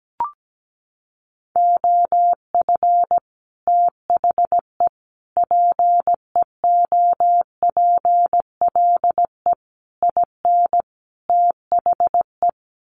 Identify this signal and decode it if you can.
Morse code, heard as a keyed tone, about 17 wpm, 700 Hz. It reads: OF THE PEOPLE IN THE
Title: Sets of 5 Words Encore 17wpm